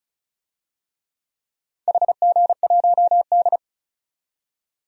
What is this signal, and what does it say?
Morse code, heard as a keyed tone, about 36 wpm, 700 Hz. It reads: HG1D
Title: DXCC-100 Call Signs - 4 Characters - 35wpm